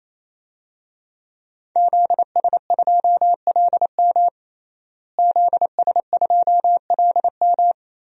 Morse code, heard as a keyed tone, about 27 wpm, 700 Hz. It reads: ZS2LM ZS2LM
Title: DXCC-100 Call Signs - 5 Characters - Repeated 3x - 28wpm